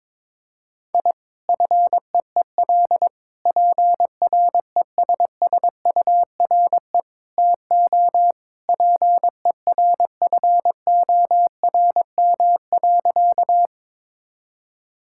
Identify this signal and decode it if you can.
Morse code, heard as a keyed tone, about 22 wpm, 700 Hz. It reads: I FEEL PRESSURE TO PERFORM.
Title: Sentences from Top 1000 Words - Review Part 6 - 22wpm